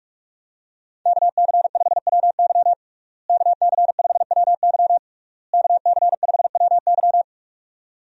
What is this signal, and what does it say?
Morse code, heard as a keyed tone, about 46 wpm, 700 Hz. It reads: KC5WY KC5WY KC5WY
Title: USA Call Signs - 5 Characters - Repeated 3x - 45wpm